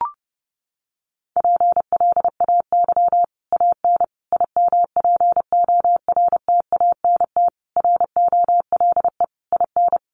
Morse code, heard as a keyed tone, about 30 wpm, 700 Hz. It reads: PLAY AN IMPORTANT ROLE IN
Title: Sets of 5 Words Encore 30wpm